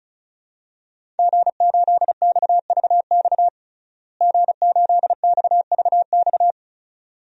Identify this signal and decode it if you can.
Morse code, heard as a keyed tone, about 34 wpm, 700 Hz. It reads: G8XVX G8XVX
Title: DXCC-100 Call Signs - 5 Characters - Repeated 2x - 35wpm